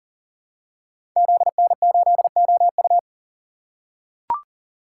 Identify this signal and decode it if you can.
Morse code, heard as a keyed tone, about 41 wpm, 700 Hz. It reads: ZN8OU
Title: Code Groups - 5 Letter:Numbers - 40wpm